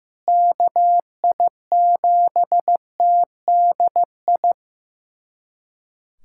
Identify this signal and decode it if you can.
Morse code, heard as a keyed tone, about 15 wpm, 700 Hz. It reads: KI7TDI
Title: USA Callsign DE Callsign - 6 Characters - 15wpm